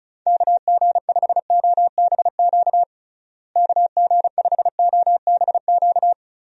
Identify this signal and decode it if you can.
Morse code, heard as a keyed tone, about 36 wpm, 700 Hz. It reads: KG5OBQ KG5OBQ
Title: USA Call Signs - 6 Characters - Repeated 2x - 35wpm